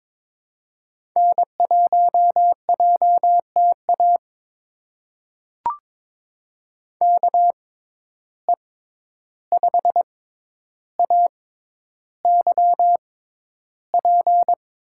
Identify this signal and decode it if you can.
Morse code, heard as a keyed tone, about 21 wpm, spaced as for 11 wpm, 700 Hz. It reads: N1JTA KE5AYP
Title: USA Call Signs - 6x Character Spacing - 22wpm